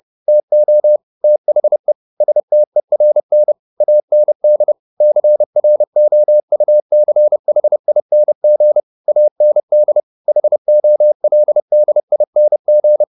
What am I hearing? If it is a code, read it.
Morse code, heard as a keyed tone, about 30 wpm, 600 Hz. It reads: TO THE STERN AND CROUCHING AND HOLDING
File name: the old man and the sea 2 30wpm